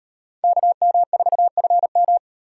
Morse code, heard as a keyed tone, about 37 wpm, 700 Hz. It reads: KM4FM